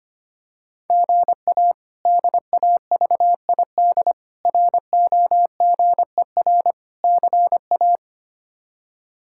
Morse code, heard as a keyed tone, about 25 wpm, 700 Hz. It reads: GA DAVID ROGER CA
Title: K1USN SST QSO Practice - 25wpm